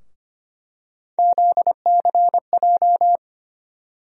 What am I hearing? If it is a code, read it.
Morse code, heard as a keyed tone, about 25 wpm, 700 Hz. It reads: ZCJ
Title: USA Call Sign Postfixes - 3 Letters 25wpm